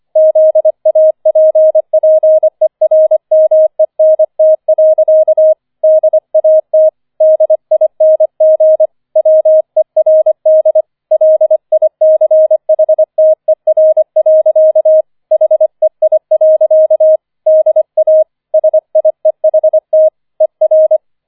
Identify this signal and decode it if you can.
Morse code, heard as a keyed tone, about 25 wpm, 630 Hz. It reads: ZAPPERMENT. DAT DING WERD LICHTER. HEI. DA SIEHT ER